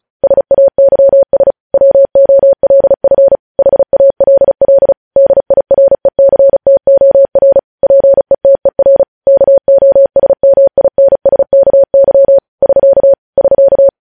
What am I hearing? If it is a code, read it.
Morse code, heard as a keyed tone, about 36 wpm, 550 Hz. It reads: SAYS WOLF HALL DIRECTOR PETER KOSMINSKY <SK> <SK>
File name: podcast / News headlines-1x-35x35